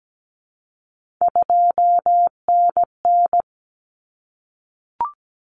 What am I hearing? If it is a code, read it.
Morse code, heard as a keyed tone, about 17 wpm, 700 Hz. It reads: 2NN
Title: Common Cut Numbers - 3-Digit 17wpm